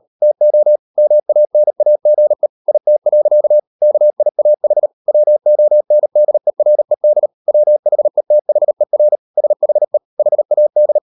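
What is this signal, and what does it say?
Morse code, heard as a keyed tone, about 38 wpm, 600 Hz. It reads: TO MANAGE IT. KIAH WONDERED WHETHER SHE HAD
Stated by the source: Morse code practice at 38 WPM, sending Ken Follett's novel Never in English, from CW med Gnister.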